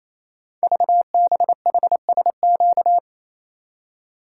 Morse code, heard as a keyed tone, about 27 wpm, 700 Hz. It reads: VBHSQ
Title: Code Groups - 5 Letter:Numbers - 28wpm